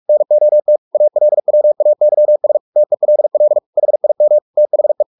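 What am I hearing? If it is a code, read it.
Morse code, heard as a keyed tone, about 44 wpm, 600 Hz. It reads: NOT ALWAYS TELL HIM THE
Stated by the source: Morse practice at 45 wpm, the novel Never by Ken Follett in English, from CW med Gnister.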